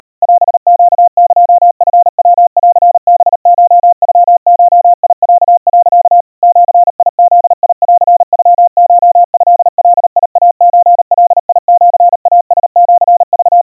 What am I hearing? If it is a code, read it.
Morse code (Japanese Wabun code), heard as a keyed tone, about 39 wpm, 700 Hz. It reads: オネエチヤンハコノゴロ、 ジブンノコトガイソガシイラシク
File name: kawa / Morse sample / kamyu 140 cpm 700 Hz Crispy